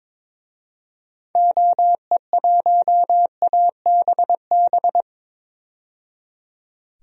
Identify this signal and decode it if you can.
Morse code, heard as a keyed tone, about 22 wpm, 700 Hz. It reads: OE1ABB
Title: DXCC-100 Call Signs - 6 Characters - 22wpm